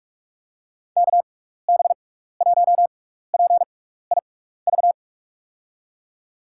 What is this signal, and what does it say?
Morse code, heard as a keyed tone, about 44 wpm, spaced as for 15 wpm, 700 Hz. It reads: KB1PIV
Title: USA Call Signs - 6 Characters - 6x Character Spacing - 45wpm